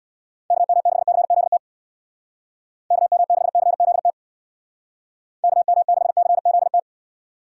Morse code, heard as a keyed tone, about 64 wpm, 700 Hz. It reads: KM6CZT KM6CZT KM6CZT